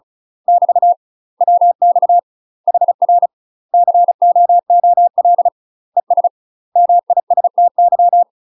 Morse code, heard as a keyed tone, about 34 wpm, 700 Hz. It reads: <BT> WX HR COOL ES MISTY
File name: UK MASTER QSOs TRAINING / 35WPM / UK MASTER QSOs 0037